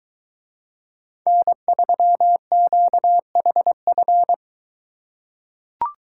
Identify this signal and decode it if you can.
Morse code, heard as a keyed tone, about 23 wpm, 700 Hz. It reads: N3QHF